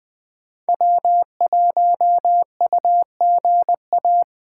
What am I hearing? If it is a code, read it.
Morse code, heard as a keyed tone, about 20 wpm, 700 Hz. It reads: W1UGA
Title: Speed Racing - USA Call Signs 17wpm